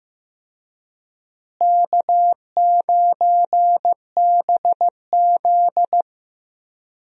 Morse code, heard as a keyed tone, about 15 wpm, 700 Hz. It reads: K9BZ
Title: USA Call Signs - 4 Characters - Repeated 3x - 15wpm